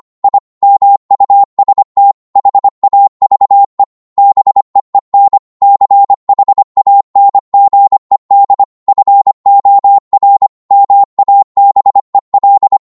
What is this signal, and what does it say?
Morse code, heard as a keyed tone, about 25 wpm, 800 Hz. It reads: I MUST HAVE BEEN CHANGED FOR MABEL